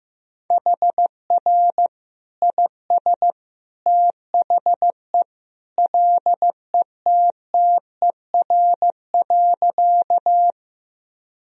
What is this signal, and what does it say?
Morse code, heard as a keyed tone, about 16 wpm, 700 Hz. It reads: HR IS THE LETTER.
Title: Sentences from Top 300 Words and Abbreviations 15wpm